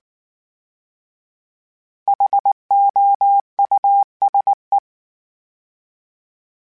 Morse code, heard as a keyed tone, about 19 wpm, 800 Hz. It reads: HOUSE
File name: General / General-mp3s / Houses @18